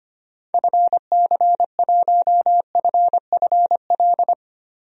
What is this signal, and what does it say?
Morse code, heard as a keyed tone, about 25 wpm, 700 Hz. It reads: FC1FFL
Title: DXCC-100 Call Signs - 6 Characters - 25wpm